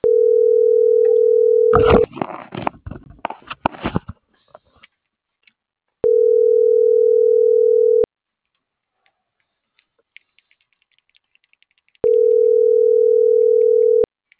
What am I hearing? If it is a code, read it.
Ringback tone (North America), 440 + 480 Hz.
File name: CENTRAL SOTOVA / out alejandro.rodriguez - .5336